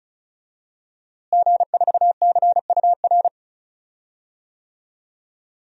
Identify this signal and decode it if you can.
Morse code, heard as a keyed tone, about 36 wpm, 700 Hz. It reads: G4CUR
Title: DXCC-100 Call Signs - 5 Characters - Repeated 2x - 35wpm